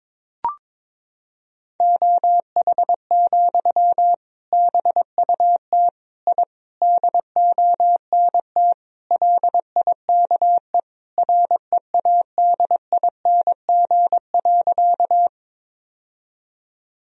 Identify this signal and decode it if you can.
Morse code, heard as a keyed tone, about 22 wpm, 700 Hz. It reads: OH, BUT I DONT LIKE READING.
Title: Sentences from Top 1000 Words - Review Part 3 - 22wpm